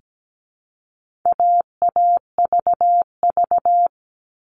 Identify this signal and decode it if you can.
Morse code, heard as a keyed tone, about 17 wpm, 700 Hz. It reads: AAVV
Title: All Cut Numbers - 4-Digit 17wpm